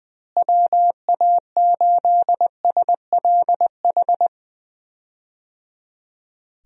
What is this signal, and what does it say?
Morse code, heard as a keyed tone, about 20 wpm, 700 Hz. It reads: WA8SLH
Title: USA Call Signs - 6 Characters - Repeated 3x - 20wpm